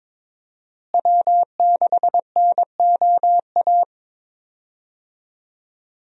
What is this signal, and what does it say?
Morse code, heard as a keyed tone, about 21 wpm, 700 Hz. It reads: W6NOA
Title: USA Call Signs - 5 Characters - Repeated 2x - 22wpm